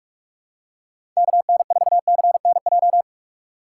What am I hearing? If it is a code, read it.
Morse code, heard as a keyed tone, about 46 wpm, 700 Hz. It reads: KN4CNJ
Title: USA Call Signs - 6 Characters - 4x Character Spacing - 45wpm